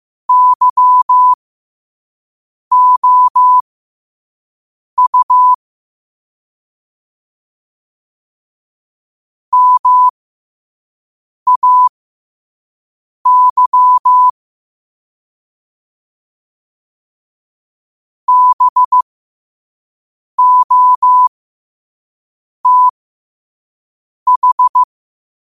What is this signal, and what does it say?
Morse code, heard as a keyed tone, about 15 wpm, spaced as for 5 wpm, 1000 Hz. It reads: YOU MAY BOTH
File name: 5wpm